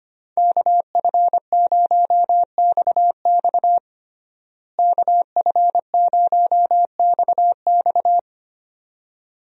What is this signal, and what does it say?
Morse code, heard as a keyed tone, about 25 wpm, 700 Hz. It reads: KF0XX KF0XX
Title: USA Call Signs - 5 Characters - Repeated 2x - 25wpm